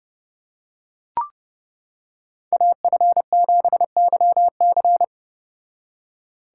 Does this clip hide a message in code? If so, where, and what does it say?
Yes: Morse code from 2.5 to 5.0 s, heard as a keyed tone, about 29 wpm, 700 Hz. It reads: AF7YC